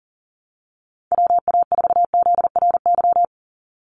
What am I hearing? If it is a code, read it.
Morse code, heard as a keyed tone, about 40 wpm, 700 Hz. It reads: WA4ZRY
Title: USA Call Signs 40wpm